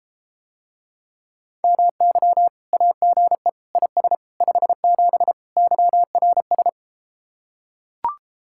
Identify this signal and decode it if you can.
Morse code, heard as a keyed tone, about 33 wpm, 700 Hz. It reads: MY AGE IS 57 YRS